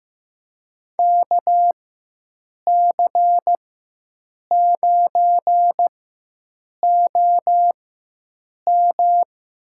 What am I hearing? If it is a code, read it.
Morse code, heard as a keyed tone, about 15 wpm, spaced as for 7 wpm, 700 Hz. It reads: KC9OM
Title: USA Call Signs - 4x Character Spacing - 15wpm